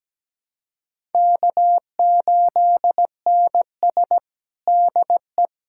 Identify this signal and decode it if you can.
Morse code, heard as a keyed tone, about 17 wpm, 700 Hz. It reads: K8NS DE
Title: USA Callsign DE Callsign - 4 Characters - 17wpm